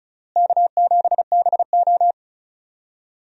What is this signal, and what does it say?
Morse code, heard as a keyed tone, about 34 wpm, 700 Hz. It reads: K7BO
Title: USA Call Signs - 4 Characters - Repeated 3x - 35wpm